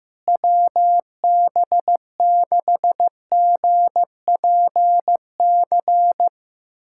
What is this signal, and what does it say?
Morse code, heard as a keyed tone, about 15 wpm, 700 Hz. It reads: WB6GPC
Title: USA Callsign DE Callsign - 6 Characters - 15wpm